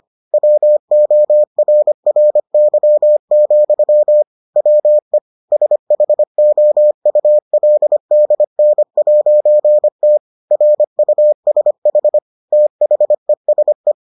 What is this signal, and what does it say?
Morse code, heard as a keyed tone, about 25 wpm, 600 Hz. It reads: WORRY, WE SHOULDN'T RUSH THESE